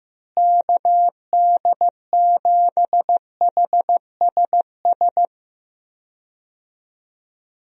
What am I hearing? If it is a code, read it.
Morse code, heard as a keyed tone, about 15 wpm, 700 Hz. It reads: KD7HSS
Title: USA Call Signs - 6 Characters - Repeated 2x - 15wpm